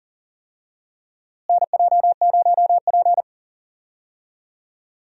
Morse code, heard as a keyed tone, about 37 wpm, 700 Hz. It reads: NJ0P
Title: USA Call Signs - 4 Characters - Repeated 3x - 40wpm